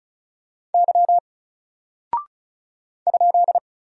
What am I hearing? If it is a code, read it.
Morse code, heard as a keyed tone, about 36 wpm, 700 Hz. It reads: Y ?
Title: Single Character (Letter:Number:Punctuation:Prosign) Alt 35wpm